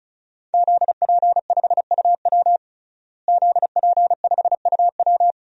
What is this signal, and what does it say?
Morse code, heard as a keyed tone, about 36 wpm, 700 Hz. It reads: ZP5UW ZP5UW
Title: DXCC-100 Call Signs - 5 Characters - Repeated 2x - 35wpm